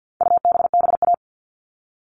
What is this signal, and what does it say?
Morse code, heard as a keyed tone, about 48 wpm, 700 Hz. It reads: UBBA
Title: All Cut Numbers - 4-Digit 50wpm